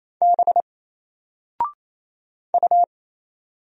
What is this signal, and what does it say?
Morse code, heard as a keyed tone, about 27 wpm, 700 Hz. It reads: B U
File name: Single-Letter-Number-28wpm